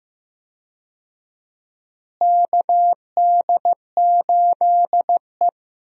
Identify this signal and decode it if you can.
Morse code, heard as a keyed tone, about 15 wpm, 700 Hz. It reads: KD8E
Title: USA Call Signs - 4 Characters - Repeated 2x - 15wpm